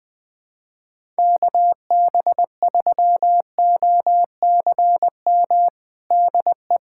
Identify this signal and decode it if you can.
Morse code, heard as a keyed tone, about 20 wpm, 700 Hz. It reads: KB3OCM DE
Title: USA Callsign DE Callsign 20wpm